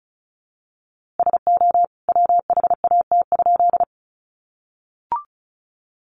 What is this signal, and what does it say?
Morse code, heard as a keyed tone, about 34 wpm, 700 Hz. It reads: SO WHAT?
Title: A Sound Of Thunder 35wpm